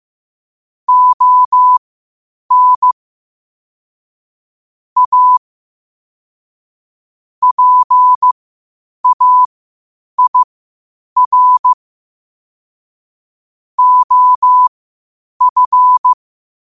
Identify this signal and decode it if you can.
Morse code, heard as a keyed tone, about 15 wpm, spaced as for 8 wpm, 1000 Hz. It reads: ON A PAIR OF
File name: qotdfiles / 8wpm